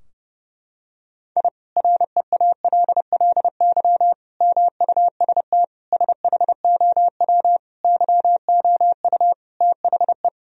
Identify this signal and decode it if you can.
Morse code, heard as a keyed tone, about 32 wpm, 700 Hz. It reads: I REALLY MUST SHOW YOU THE
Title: Sentences from Top 1000 Words - Review Part 4 - 30wpm